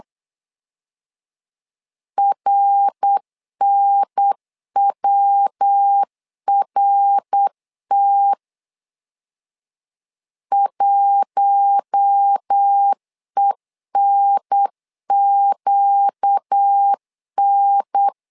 Morse code, heard as a keyed tone, about 8 wpm, 800 Hz. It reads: RNWRT 1ENQN
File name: LEZIONI / 4 Q1 8wpm 3035